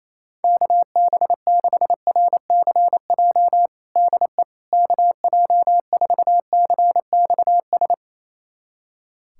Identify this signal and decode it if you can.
Morse code, heard as a keyed tone, about 28 wpm, 700 Hz. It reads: KB6RCJ DE KJ4CXS
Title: USA Callsign DE Callsign - 6 Characters - 28wpm